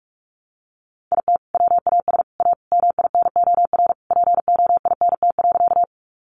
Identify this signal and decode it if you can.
Morse code, heard as a keyed tone, about 44 wpm, 700 Hz. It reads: IT WAS A MINOR POINT.
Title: Sentences from Top 1500 Words 45wpm